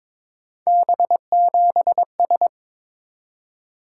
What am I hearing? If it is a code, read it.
Morse code, heard as a keyed tone, about 22 wpm, 700 Hz. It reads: B7S
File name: practice / 22wpm / Three Letter-Number Combinations - 22wpm